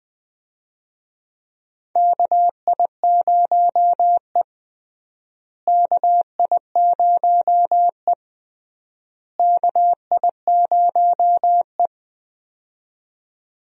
Morse code, heard as a keyed tone, about 20 wpm, 700 Hz. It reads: KI0E KI0E KI0E
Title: USA Call Signs - 4 Characters - Repeated 3x - 20wpm